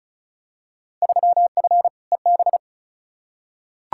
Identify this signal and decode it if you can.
Morse code, heard as a keyed tone, about 36 wpm, 700 Hz. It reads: 3F EB
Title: Field Day Exchanges - 35wpm